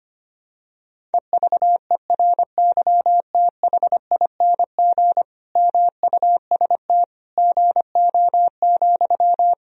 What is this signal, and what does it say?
Morse code, heard as a keyed tone, about 25 wpm, 700 Hz. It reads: EVERYTHING MUST GO,
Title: Sentences from Top 1000 Words - Review Part 4 - 25wpm